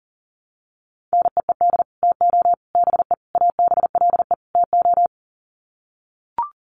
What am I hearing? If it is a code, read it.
Morse code, heard as a keyed tone, about 40 wpm, 700 Hz. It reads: NEED TO BE ABLE TO
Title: Sets of 5 Words Encore 40wpm